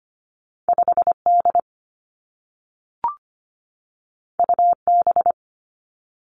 Morse code, heard as a keyed tone, about 25 wpm, 700 Hz. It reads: 5 D U B
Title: Two Challenging Letter-Number Combinations 25wpm